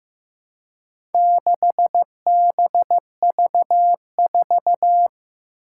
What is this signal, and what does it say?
Morse code, heard as a keyed tone, about 15 wpm, 700 Hz. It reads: 6BV4
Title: Four Challenging Letter-Number Combinations 15wpm